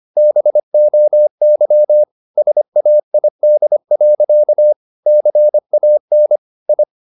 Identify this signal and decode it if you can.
Morse code, heard as a keyed tone, about 25 wpm, 600 Hz. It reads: BOY SAID. CAN I